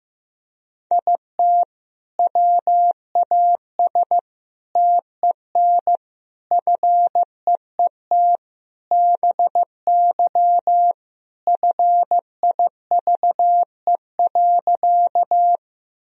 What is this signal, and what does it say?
Morse code, heard as a keyed tone, about 15 wpm, 700 Hz. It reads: IT WAS TEN FEET BY FIVE.
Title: Sentences from Top 1000 Words - Review Part 2 - 15wpm